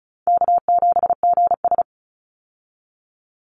Morse code, heard as a keyed tone, about 34 wpm, 700 Hz. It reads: K7GS
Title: Easy Call Signs Repeated 2x - 35wpm